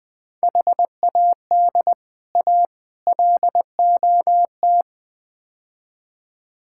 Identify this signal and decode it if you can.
Morse code, heard as a keyed tone, about 21 wpm, 700 Hz. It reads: HAD A LOT
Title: Build A Phrase - 4-Word Phrases - 20wpm